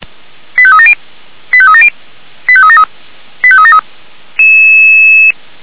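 Five-tone selective call (ZVEI), heard as quick runs of five tones, tones of about 70 ms. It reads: 85370 85370 85373 85373
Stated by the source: Five-tone ZVEI selective call received on SDR.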